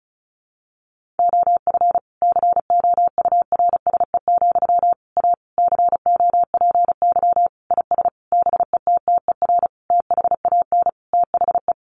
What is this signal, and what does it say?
Morse code, heard as a keyed tone, about 34 wpm, 700 Hz. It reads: OF COURSE, A COPY IS BETTER THAN THE
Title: Sentences from Top 1400 Words 35wpm